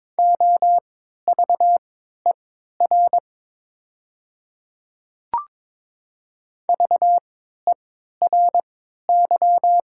Morse code, heard as a keyed tone, about 22 wpm, spaced as for 13 wpm, 700 Hz. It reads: OVER VERY
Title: 3x Character Spacing - Top 200 Words - 22wpm